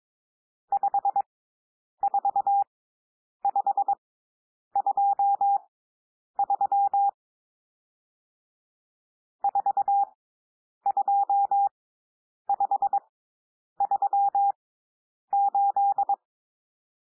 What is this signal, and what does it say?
Morse code, heard as a keyed tone, about 22 wpm, spaced as for 9 wpm, 800 Hz. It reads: 54523 42538
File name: cwmp3 / N1 SOUND NUMBERS 22 wpm